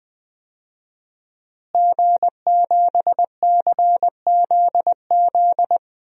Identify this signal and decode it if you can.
Morse code, heard as a keyed tone, about 20 wpm, 700 Hz. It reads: G7CZZ
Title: DXCC-100 Call Signs - 5 Characters - Repeated 2x - 20wpm